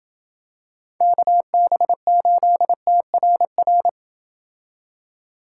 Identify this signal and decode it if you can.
Morse code, heard as a keyed tone, about 27 wpm, 700 Hz. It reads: KB8TRR